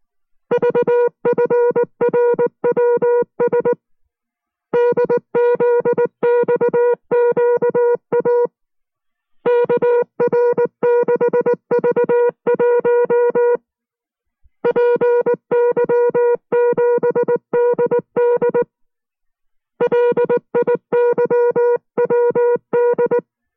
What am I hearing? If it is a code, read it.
Morse code, heard as a keyed tone, about 19 wpm, 480 Hz. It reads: VFRWS DZXQA KR6V1 PY7DD LIYWD